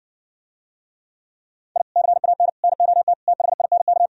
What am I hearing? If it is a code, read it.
Morse code, heard as a keyed tone, about 69 wpm, 700 Hz. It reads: I CAN NOT THINK